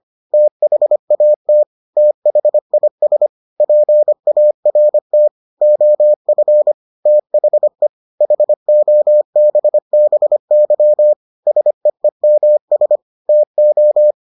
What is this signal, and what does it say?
Morse code, heard as a keyed tone, about 25 wpm, 600 Hz. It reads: THAT THIS PART OF THE HOBBY SEEMS TO